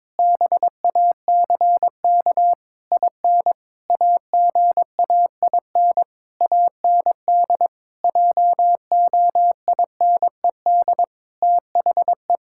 Morse code, heard as a keyed tone, about 22 wpm, 700 Hz. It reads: BACK IN AGAIN AND JOINED THE
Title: Pangram Challenge v2 - 22wpm